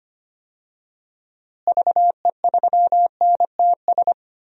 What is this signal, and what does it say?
Morse code, heard as a keyed tone, about 25 wpm, 700 Hz. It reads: VE3NTS
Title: DXCC-100 Call Signs - 6 Characters - Repeated 3x - 25wpm